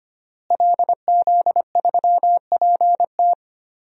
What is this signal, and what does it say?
Morse code, heard as a keyed tone, about 25 wpm, 700 Hz. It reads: LZ3PT
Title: DXCC-100 Call Signs - 5 Characters - Repeated 3x - 25wpm